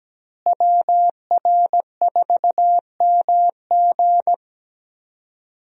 Morse code, heard as a keyed tone, about 17 wpm, 700 Hz. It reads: WR4MG
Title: USA Call Signs - 5 Characters - Repeated 3x - 17wpm